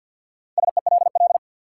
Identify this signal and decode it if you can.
Morse code, heard as a keyed tone, about 51 wpm, 700 Hz. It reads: SELL
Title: Sentences from Top 1100 Words - New Word Drill Alt - 50wpm